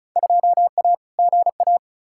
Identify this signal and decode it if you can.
Morse code, heard as a keyed tone, about 34 wpm, 700 Hz. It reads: 2A GA
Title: Field Day Exchanges - 35wpm